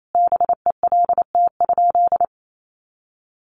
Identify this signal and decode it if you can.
Morse code, heard as a keyed tone, about 28 wpm, 700 Hz. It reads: BELT?
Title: Sentences from Top 1400 Words 28wpm